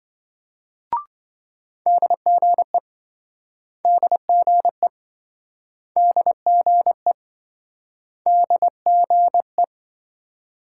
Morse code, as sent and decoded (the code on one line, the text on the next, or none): -.. --. / -.. --. / -.. --. / -.. --. .
DGE DGE DGE DGE